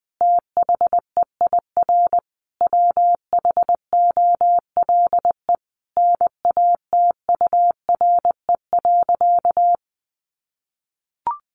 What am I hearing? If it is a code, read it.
Morse code, heard as a keyed tone, about 20 wpm, 700 Hz. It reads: THEIR WHOLE NATURE.
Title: Sentences from Top 1500 Words 20wpm